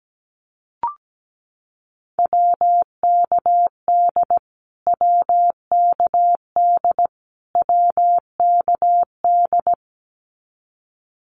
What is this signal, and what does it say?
Morse code, heard as a keyed tone, about 17 wpm, 700 Hz. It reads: WKD WKD WKD
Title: Abbreviations Repeated 3x - 17wpm